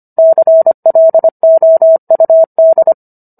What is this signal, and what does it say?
Morse code, heard as a keyed tone, about 25 wpm, 650 Hz. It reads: CLOUD